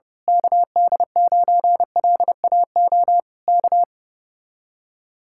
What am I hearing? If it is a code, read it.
Morse code, heard as a keyed tone, about 29 wpm, 700 Hz. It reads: KD9LAO K
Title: Speed-Racing Common QSO Phrases Encore - 25wpm